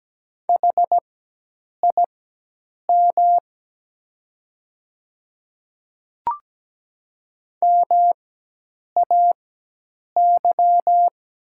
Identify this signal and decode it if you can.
Morse code, heard as a keyed tone, about 17 wpm, spaced as for 8 wpm, 700 Hz. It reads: HIM MAY
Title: 4x Character Spacing - Top 100 Words - 17wpm